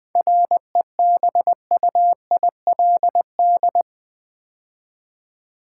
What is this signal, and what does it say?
Morse code, heard as a keyed tone, about 20 wpm, 700 Hz. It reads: REBUILD
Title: 7 Letter Words - First 3 Letters Given - 20wpm